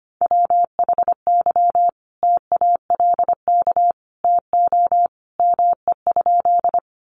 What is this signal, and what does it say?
Morse code, heard as a keyed tone, about 25 wpm, 700 Hz. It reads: WHY TALK TO ME?
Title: Sentences from Top 400 Words 25wpm